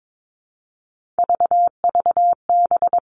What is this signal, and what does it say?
Morse code, heard as a keyed tone, about 22 wpm, 700 Hz. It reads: VVB
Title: Three Challenging Letter-Number Combinations 22wpm